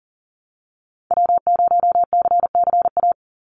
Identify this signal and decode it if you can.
Morse code, heard as a keyed tone, about 40 wpm, 700 Hz. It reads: W0CCA
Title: NA SOTA Activator Callsigns 40wpm